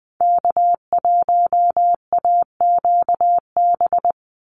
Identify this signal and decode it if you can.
Morse code, heard as a keyed tone, about 20 wpm, 700 Hz. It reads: K1AQB
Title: Call Signs Repeated 3x - 20wpm